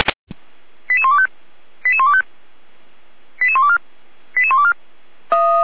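Five-tone selective call (ZVEI), heard as quick runs of five tones, tones of about 70 ms. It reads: 88125 88125 88124 88124
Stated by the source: ZVEI five-tone selective call received on SDR.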